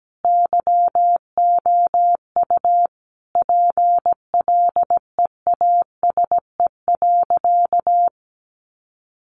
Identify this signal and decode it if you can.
Morse code, heard as a keyed tone, about 17 wpm, 700 Hz. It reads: YOU PLEASE.
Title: Sentences from Top 1500 Words 17wpm